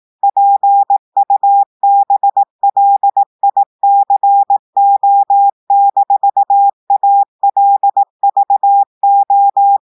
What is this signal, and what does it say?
Morse code, heard as a keyed tone, about 18 wpm, 800 Hz. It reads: PUBLICO-ALVO
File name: audios / portugais / portugais texte en clair 2 vitesse 18 mots minutes